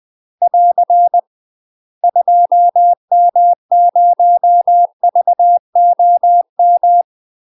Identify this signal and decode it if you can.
Morse code, heard as a keyed tone, about 20 wpm, 700 Hz. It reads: <AR> 2M0VOM
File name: UK MASTER QSOs TRAINING / 20WPM / UK MASTER QSOs 0052